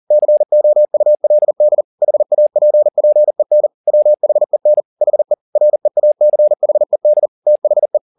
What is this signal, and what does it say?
Morse code, heard as a keyed tone, about 40 wpm, 600 Hz. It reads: COULD HAPPEN WHEN HE REACHED THE
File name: the old man and the sea 2 40wpm